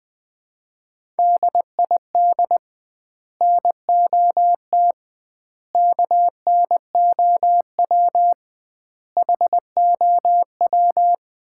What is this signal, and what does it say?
Morse code, heard as a keyed tone, about 21 wpm, 700 Hz. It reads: DID NOT KNOW HOW
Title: 2x Word-Spacing - Sets of 5 Words - 20wpm